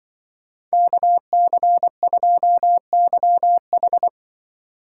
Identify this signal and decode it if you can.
Morse code, heard as a keyed tone, about 24 wpm, 700 Hz. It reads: KC2YH
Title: Speed Racing - USA 2x2 Call Signs 20wpm